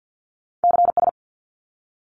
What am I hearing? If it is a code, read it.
Morse code, heard as a keyed tone, about 50 wpm, 700 Hz. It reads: CS
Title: Abbreviations 50wpm